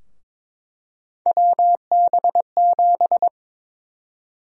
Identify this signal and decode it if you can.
Morse code, heard as a keyed tone, about 22 wpm, 700 Hz. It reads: WB7
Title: USA Call Sign Prefixes - 22wpm